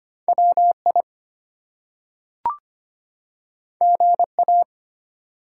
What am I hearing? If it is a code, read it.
Morse code, heard as a keyed tone, about 25 wpm, 700 Hz. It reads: WI GA
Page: morse-code-ninja-practice-25wpm